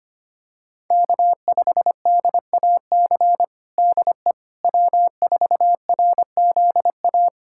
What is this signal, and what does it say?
Morse code, heard as a keyed tone, about 25 wpm, 700 Hz. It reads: K5DAC DE W4RZA
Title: Practice QSO - Exchange 4 - Ending - 25wpm